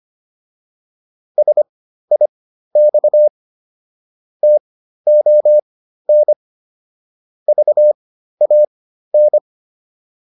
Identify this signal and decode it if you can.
Morse code, heard as a keyed tone, about 25 wpm, spaced as for 13 wpm, 600 Hz. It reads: SIX TON VAN
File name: PR202 13